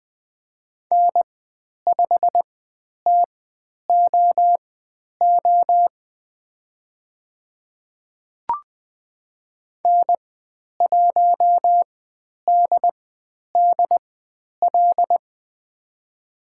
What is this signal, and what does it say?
Morse code, heard as a keyed tone, about 20 wpm, spaced as for 10 wpm, 700 Hz. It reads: N5TOO N1DDL
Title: 102 Lesson 17g - Letter u - Call Signs 20wpm